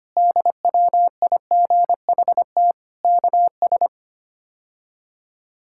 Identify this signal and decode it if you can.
Morse code, heard as a keyed tone, about 25 wpm, 700 Hz. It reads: DWIGHT KS
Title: K1USN SST QSO Practice - 25wpm